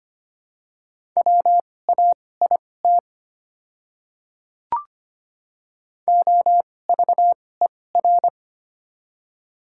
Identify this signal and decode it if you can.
Morse code, heard as a keyed tone, about 25 wpm, spaced as for 18 wpm, 690 Hz. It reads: WAIT OVER
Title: 2x Character Spacing - Top 500 Words - 25wpm